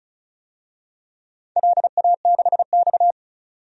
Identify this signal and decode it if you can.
Morse code, heard as a keyed tone, about 36 wpm, 690 Hz. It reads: LA6X
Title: DXCC-100 Call Signs - 4 Characters - 35wpm